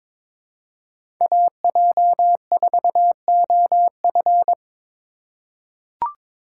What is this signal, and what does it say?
Morse code, heard as a keyed tone, about 22 wpm, 700 Hz. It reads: AJ4OF
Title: USA Call Signs - 5 Characters - 4x Character Spacing - 22wpm